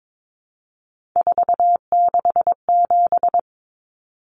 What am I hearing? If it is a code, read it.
Morse code, heard as a keyed tone, about 22 wpm, 700 Hz. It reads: 467
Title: Numbers - 3-Digit 22wpm